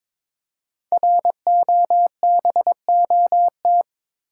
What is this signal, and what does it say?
Morse code, heard as a keyed tone, about 22 wpm, 700 Hz. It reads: ROBOT
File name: 5 Letter Words - First 3 Letters Given - 22wpm